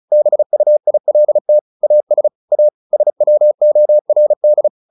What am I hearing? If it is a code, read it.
Morse code, heard as a keyed tone, about 34 wpm, 600 Hz. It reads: BUILT AS A SWORD
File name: the old man and the sea 2 35wpm